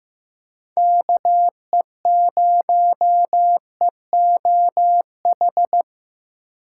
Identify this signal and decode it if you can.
Morse code, heard as a keyed tone, about 15 wpm, 700 Hz. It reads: KE0EOH
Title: USA Call Signs - 6 Characters - 6x Character Spacing - 15wpm